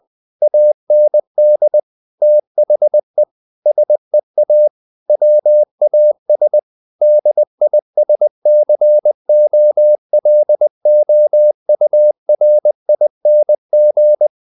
Morse code, heard as a keyed tone, about 20 wpm, 600 Hz. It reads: AND THE SEA WAS DISCOLOURING
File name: the old man and the sea part 2 20wpm